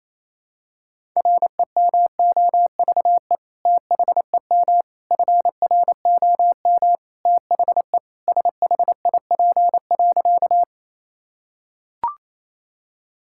Morse code, heard as a keyed tone, about 28 wpm, 700 Hz. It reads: REMOVE THEM FROM THE SHIP.
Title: Sentences from Top 1000 Words - Review Part 5 - 28wpm